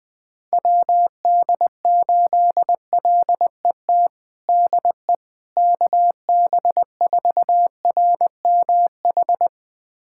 Morse code, heard as a keyed tone, about 21 wpm, 700 Hz. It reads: WD8LET DE KB4RMH
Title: USA Callsign DE Callsign 20wpm